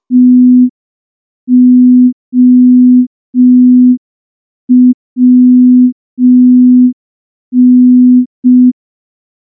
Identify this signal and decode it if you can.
Morse code, heard as a keyed tone, about 5 wpm, 250 Hz. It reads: TOWN
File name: morse / morse-training-05WPM